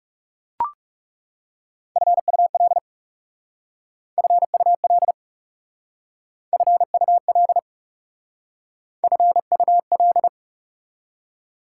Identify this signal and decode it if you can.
Morse code, heard as a keyed tone, about 37 wpm, 690 Hz. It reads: FUL FUL FUL FUL
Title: Speed-Racing Prefixes and Suffixes 30wpm